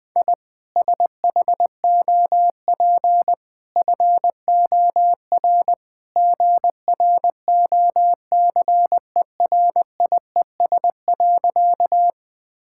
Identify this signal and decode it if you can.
Morse code, heard as a keyed tone, about 20 wpm, 700 Hz. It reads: I SHOP FOR GROCERIES.
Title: Easy Sentences with Hints - 20wpm